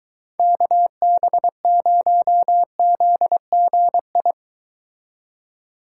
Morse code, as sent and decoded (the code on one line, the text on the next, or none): -.- -... ----- --.. --. ..
KB0ZGI